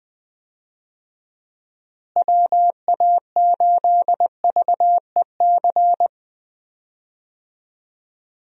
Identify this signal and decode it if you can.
Morse code, heard as a keyed tone, about 21 wpm, 700 Hz. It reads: WA8VEC
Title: USA Call Signs - 6 Characters - Repeated 2x - 20wpm